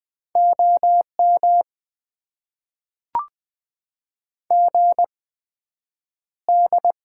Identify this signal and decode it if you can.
Morse code, heard as a keyed tone, about 20 wpm, spaced as for 9 wpm, 700 Hz. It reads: OM GD
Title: Top 90 QSO Elements - 8x Character Spacing - 20wpm